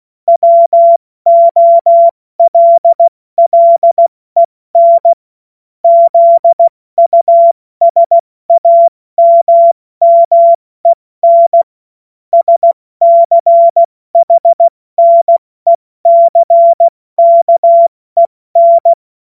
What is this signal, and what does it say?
Morse code, heard as a keyed tone, about 16 wpm, 680 Hz. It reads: WOLLEN ZUSAMMEN SCHNECKEN
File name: Witz-23-September-2025-70-F80